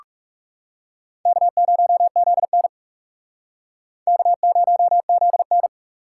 Morse code, heard as a keyed tone, about 40 wpm, 700 Hz. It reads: K0ZN K0ZN